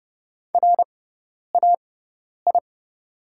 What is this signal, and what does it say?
Morse code, heard as a keyed tone, about 29 wpm, spaced as for 10 wpm, 700 Hz. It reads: RAI
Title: 6x Character Spacing - Top 300 Words - 30wpm